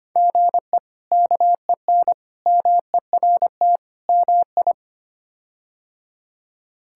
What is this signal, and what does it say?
Morse code, heard as a keyed tone, about 25 wpm, 700 Hz. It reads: GE KEN MERT MI
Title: K1USN SST QSO Practice - 25wpm